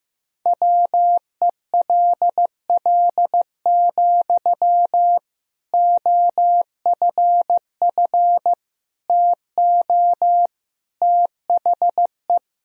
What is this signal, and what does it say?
Morse code, heard as a keyed tone, about 15 wpm, 700 Hz. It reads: WELL, OFF TO THE
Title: Sentences from Top 1000 Words - Review Part 3 - 15wpm